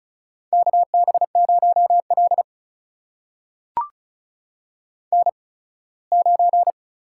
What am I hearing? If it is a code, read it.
Morse code, heard as a keyed tone, about 36 wpm, spaced as for 21 wpm, 700 Hz. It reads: KB0L N9
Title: USA Call Signs - 4 Characters - 8x Character Spacing - 35wpm